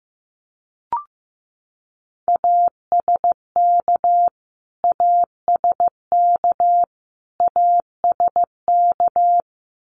Morse code, heard as a keyed tone, about 15 wpm, 700 Hz. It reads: ASK ASK ASK
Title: Top 500 Words Repeated 15wpm